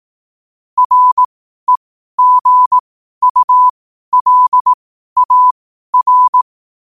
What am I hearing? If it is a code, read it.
Morse code, heard as a keyed tone, about 18 wpm, spaced as for 12 wpm, 1000 Hz. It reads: REGULAR
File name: qotdfiles / 12wpm